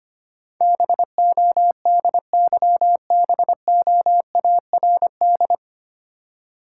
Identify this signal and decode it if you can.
Morse code, heard as a keyed tone, about 25 wpm, 700 Hz. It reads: BODYBOARD
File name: practice / 25wpm / Difficult Words - Speed-Racing with Hints - 25wpm